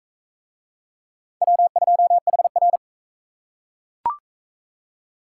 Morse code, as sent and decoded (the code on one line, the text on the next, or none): .-- ..--- .... .-.
W2HR